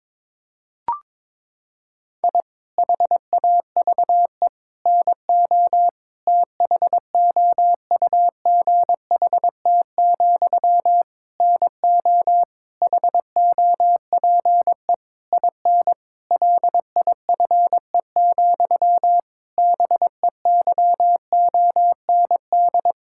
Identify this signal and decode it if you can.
Morse code, heard as a keyed tone, about 22 wpm, 700 Hz. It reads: I HAVE NO THOUGHT, NO HOPE IN LIFE, BEYOND
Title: Sentences from Top 1000 Words - Review Part 6 - 22wpm